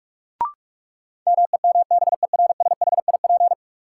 Morse code, heard as a keyed tone, about 46 wpm, 700 Hz. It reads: MEMBERSHIP
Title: Sentences from Top 1400 Words - New Word Drill - 45wpm